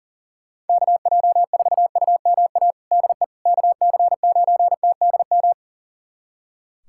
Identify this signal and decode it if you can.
Morse code, heard as a keyed tone, about 40 wpm, 700 Hz. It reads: KJ4UMA DE KC9TDM
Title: USA Callsign DE Callsign - 6 Characters - 40wpm